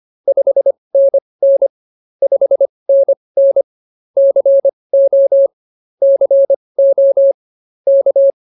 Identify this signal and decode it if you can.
Morse code, heard as a keyed tone, about 25 wpm, spaced as for 20 wpm, 590 Hz. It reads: 5NN 5NN CO CO K